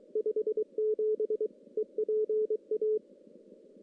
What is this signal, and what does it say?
Morse code, heard as a keyed tone, about 23 wpm, 430 Hz. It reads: 57 EPA
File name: contests / CW-SS-k5tr- -1600